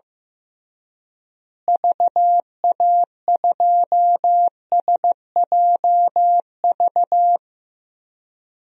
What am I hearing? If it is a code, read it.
Morse code, heard as a keyed tone, about 15 wpm, 700 Hz. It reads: VA2SJV
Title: DXCC-100 Call Signs - 6 Characters - Repeated 2x - 15wpm